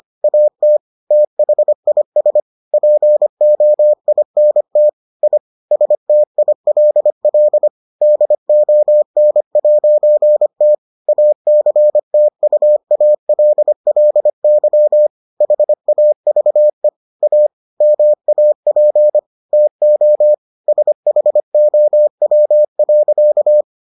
Morse code, heard as a keyed tone, about 25 wpm, 600 Hz. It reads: AT THIS POINT I STILL DON'T ACTUALLY HAVE A MAP TO SHOW.